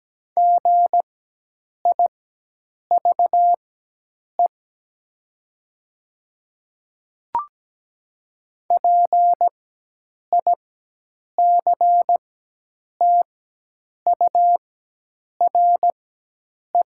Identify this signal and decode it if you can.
Morse code, heard as a keyed tone, about 17 wpm, spaced as for 8 wpm, 700 Hz. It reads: GIVE PICTURE
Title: 4x Character Spacing - Top 200 Words - 17wpm